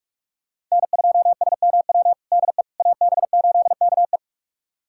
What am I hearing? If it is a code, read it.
Morse code, heard as a keyed tone, about 48 wpm, 700 Hz. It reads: N2SMW DE AB8KE
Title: USA Callsign DE Callsign - 5 Characters - 45wpm